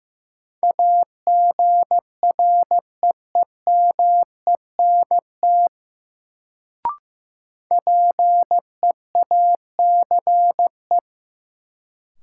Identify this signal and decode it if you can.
Morse code, heard as a keyed tone, about 15 wpm, 700 Hz. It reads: AGREEMENT PEACE
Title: Sentences from Top 1500 Words - New Word Drill - 15wpm